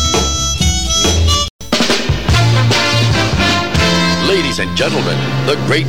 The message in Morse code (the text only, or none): none